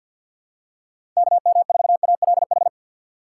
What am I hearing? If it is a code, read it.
Morse code, heard as a keyed tone, about 52 wpm, 700 Hz. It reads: KM4ALH
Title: USA Call Signs - 6 Characters - 8x Character Spacing - 50wpm